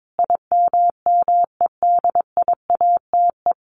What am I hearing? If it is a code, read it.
Morse code, heard as a keyed tone, about 22 wpm, 700 Hz. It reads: IMMEDIATE
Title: Sentences from Top 1200 Words 22wpm